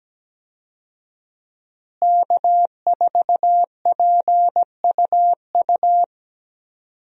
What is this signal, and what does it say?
Morse code, heard as a keyed tone, about 17 wpm, 700 Hz. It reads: K4PUU
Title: USA Call Signs - 5 Characters - Repeated 3x - 17wpm